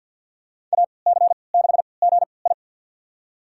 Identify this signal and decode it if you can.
Morse code, heard as a keyed tone, about 47 wpm, spaced as for 28 wpm, 700 Hz. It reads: AC6GI
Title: USA Call Signs - 5 Characters - 3x Character Spacing - 50wpm